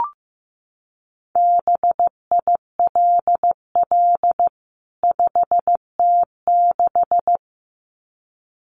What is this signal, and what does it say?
Morse code, heard as a keyed tone, about 15 wpm, 700 Hz. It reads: BILL 5T6
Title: CWT Exchange 15wpm